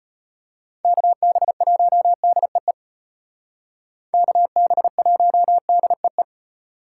Morse code, heard as a keyed tone, about 33 wpm, 700 Hz. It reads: KB1DEE KB1DEE